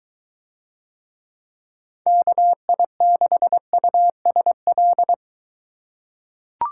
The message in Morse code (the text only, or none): KI6USL